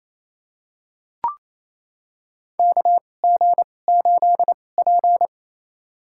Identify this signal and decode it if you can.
Morse code, heard as a keyed tone, about 28 wpm, spaced as for 20 wpm, 690 Hz. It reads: KG8P
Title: USA Call Signs - 2x Character Spacing - 28wpm